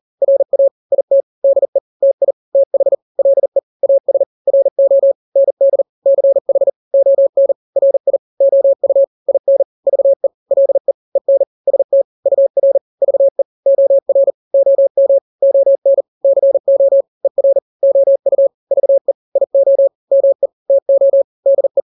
Morse code, heard as a keyed tone, about 40 wpm, 550 Hz. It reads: RA IT DE TI TH LE AS RO ND CH ON RI OU IN VE LE EN ST UR VE OR OM ON CO ER OU VE IO ME TO DE
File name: n-grams / 2-grams 40 35 WPM